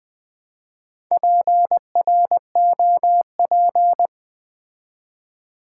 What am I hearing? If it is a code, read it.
Morse code, heard as a keyed tone, about 20 wpm, 700 Hz. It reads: PROP
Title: Build A Word - 2 Letters At A Time - 8-Letter Words - 20wpm